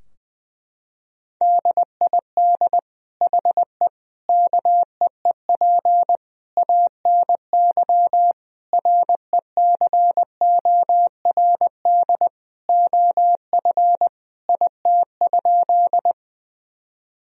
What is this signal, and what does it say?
Morse code, heard as a keyed tone, about 20 wpm, 700 Hz. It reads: DID HE KEEP ANY RECORD OF IT?
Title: Sentences from Top 1000 Words - Review Part 3 - 20wpm